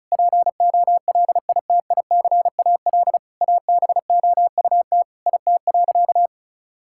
Morse code, heard as a keyed tone, about 36 wpm, 700 Hz. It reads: POLITICAL ABOUT IT.
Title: Sentences from Top 1000 Words - Review Part 5 - 35wpm